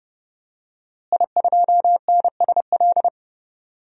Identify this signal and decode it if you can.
Morse code, heard as a keyed tone, about 31 wpm, 700 Hz. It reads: I2NSL
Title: DXCC-100 Call Signs - 5 Characters - Repeated 2x - 30wpm